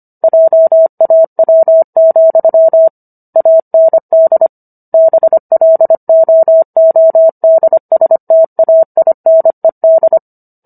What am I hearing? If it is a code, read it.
Morse code, heard as a keyed tone, about 25 wpm, 650 Hz. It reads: JAW, AND BLOODSTAINED